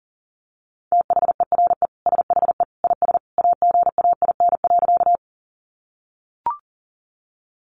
Morse code, heard as a keyed tone, about 40 wpm, 700 Hz. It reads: THERE SHE IS AGAIN.
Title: Sentences from Top 200 Words 40wpm